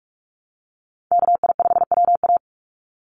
Morse code, heard as a keyed tone, about 44 wpm, 700 Hz. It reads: KI5WA
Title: NA SOTA Chaser Callsigns 45wpm